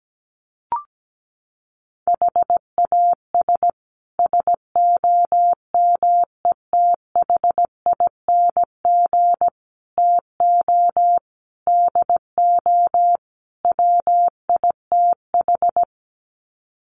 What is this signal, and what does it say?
Morse code, heard as a keyed tone, about 17 wpm, 700 Hz. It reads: HAS SOMETHING TO DO WITH
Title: Sets of 5 Words Encore 17wpm